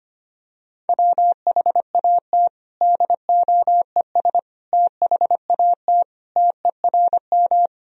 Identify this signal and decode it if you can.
Morse code, heard as a keyed tone, about 25 wpm, 700 Hz. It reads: WHAT DOES THAT TERM
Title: Sentences from Top 1000 Words - Review Part 4 - 25wpm